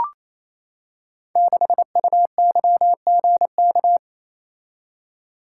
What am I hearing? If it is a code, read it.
Morse code, heard as a keyed tone, about 29 wpm, 700 Hz. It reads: 6UYGK
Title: Code Groups - 5 Letter:Numbers - 28wpm